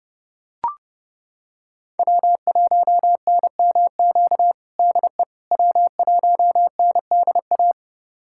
Morse code, heard as a keyed tone, about 30 wpm, 700 Hz. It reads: W1NMQ DE W1NDA
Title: USA Callsign DE Callsign 30wpm